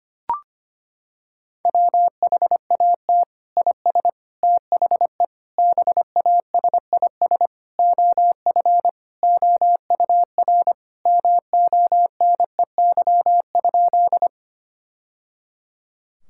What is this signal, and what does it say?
Morse code, heard as a keyed tone, about 25 wpm, 700 Hz. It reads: WHAT IS THE BASIS OF OUR MONEY?
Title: Sentences from Top 1000 Words - Review Part 5 - 25wpm